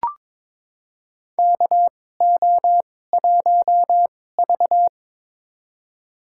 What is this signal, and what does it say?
Morse code, heard as a keyed tone, about 22 wpm, spaced as for 16 wpm, 700 Hz. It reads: KO1V
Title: USA Call Signs - 4 Characters - 2x Character Spacing - 22wpm